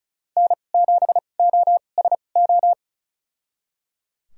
Morse code, heard as a keyed tone, about 36 wpm, spaced as for 25 wpm, 700 Hz. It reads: N7OSO
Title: USA Call Signs - 5 Characters - 2x Character Spacing - 35wpm